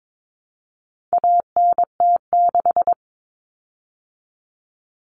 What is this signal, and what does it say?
Morse code, heard as a keyed tone, about 22 wpm, 700 Hz. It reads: ANT6
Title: Common Cut Numbers - 4-Digit 22wpm